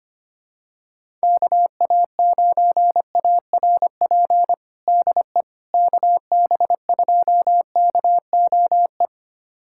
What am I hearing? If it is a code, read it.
Morse code, heard as a keyed tone, about 25 wpm, 700 Hz. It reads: KA9ARP DE KB2KOE
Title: USA Callsign DE Callsign - 6 Characters - 25wpm